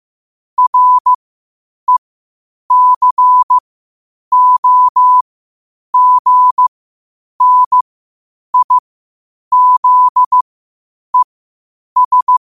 Morse code, heard as a keyed tone, about 15 wpm, spaced as for 8 wpm, 1000 Hz. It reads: RECOGNIZES
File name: qotdfiles / 8wpm